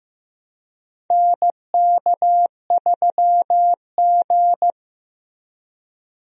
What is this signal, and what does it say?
Morse code, heard as a keyed tone, about 15 wpm, 700 Hz. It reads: NK3G
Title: USA Call Signs - 4 Characters - Repeated 2x - 15wpm